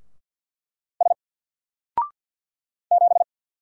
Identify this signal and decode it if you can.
Morse code, heard as a keyed tone, about 48 wpm, 700 Hz. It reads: S 7
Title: Single Letter-Number 50wpm